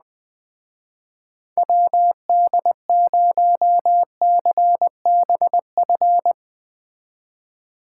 Morse code, heard as a keyed tone, about 21 wpm, 700 Hz. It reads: WD0CBF
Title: Speed Racing - USA Call Signs 17wpm